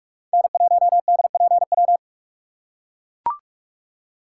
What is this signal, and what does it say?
Morse code, heard as a keyed tone, about 42 wpm, 700 Hz. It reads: N1DPW
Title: USA Call Signs - 5 Characters - 3x Character Spacing - 45wpm